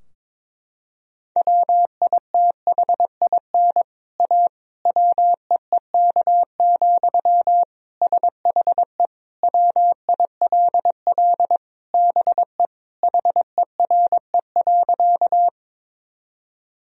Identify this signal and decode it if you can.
Morse code, heard as a keyed tone, about 22 wpm, 700 Hz. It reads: WITHIN A WEEK, SHE WILL BE HERE.
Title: Sentences from Top 1000 Words - Review Part 4 - 22wpm